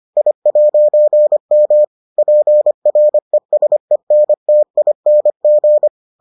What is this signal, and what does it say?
Morse code, heard as a keyed tone, about 25 wpm, 600 Hz. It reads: I'M PRESENTING